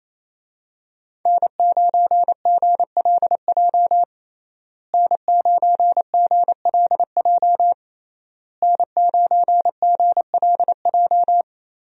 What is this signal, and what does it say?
Morse code, heard as a keyed tone, about 28 wpm, 700 Hz. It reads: N9GLJ N9GLJ N9GLJ
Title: USA Call Signs - 5 Characters - Repeated 3x - 28wpm